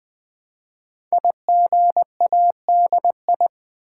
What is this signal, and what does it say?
Morse code, heard as a keyed tone, about 20 wpm, 700 Hz. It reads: IGADI
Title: Code Groups - 5 Letters - 20wpm